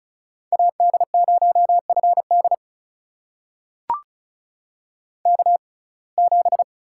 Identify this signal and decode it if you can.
Morse code, heard as a keyed tone, about 35 wpm, spaced as for 25 wpm, 700 Hz. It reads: AD0FD K7
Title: USA Call Signs - 5 Characters - 6x Character Spacing - 35wpm